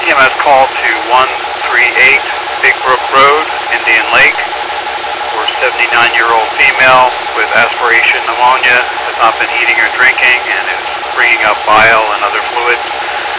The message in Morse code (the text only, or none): none